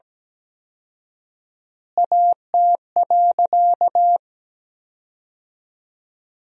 Morse code, heard as a keyed tone, about 17 wpm, 700 Hz. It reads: AT.
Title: 4x Word-Spacing - Sentences from Top 200 Words - 17wpm